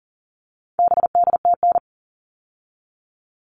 Morse code, heard as a keyed tone, about 40 wpm, 700 Hz. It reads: BDTN
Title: All Cut Numbers - 4-Digit 40wpm